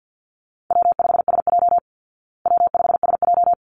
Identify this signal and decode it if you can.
Morse code, heard as a keyed tone, about 48 wpm, 700 Hz. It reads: W5SJ W5SJ
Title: Easy Call Signs Repeated 2x - 50wpm